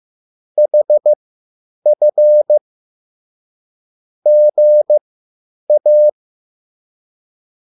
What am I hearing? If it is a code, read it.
Morse code, heard as a keyed tone, about 15 wpm, spaced as for 9 wpm, 600 Hz. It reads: HF GA
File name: QSL / CW lesson